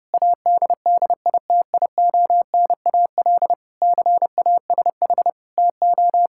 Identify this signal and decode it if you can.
Morse code, heard as a keyed tone, about 32 wpm, 700 Hz. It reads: ADDITIONAL CASH TO
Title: Sentences from Top 1000 Words - Review Part 6 - 30wpm